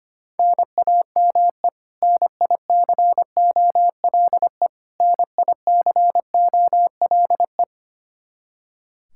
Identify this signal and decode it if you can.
Morse code, heard as a keyed tone, about 25 wpm, 700 Hz. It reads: NAME NICOLE NICOLE
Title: Practice QSO - Exchange 1 - Beginning - 25wpm